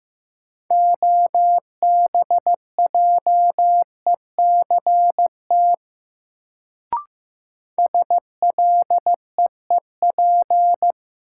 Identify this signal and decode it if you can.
Morse code, heard as a keyed tone, about 15 wpm, 700 Hz. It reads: OBJECT SLEEP
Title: Sentences from Top 500 Words - New Word Drill - 15wpm